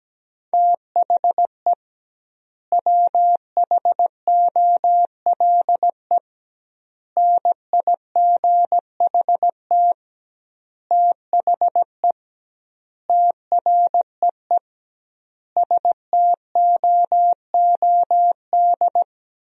Morse code, heard as a keyed tone, about 17 wpm, 700 Hz. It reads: THE WHOLE NIGHT THE TREE STOOD
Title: 2x Word-Spacing - Sentences from Top 500 Words - 17wpm